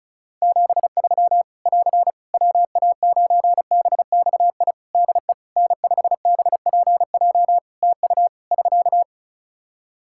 Morse code, heard as a keyed tone, about 36 wpm, 700 Hz. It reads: 73 <AR> WA9BXI DE N5BPJ TU <SK>
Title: K1USN SST QSO Practice 35wpm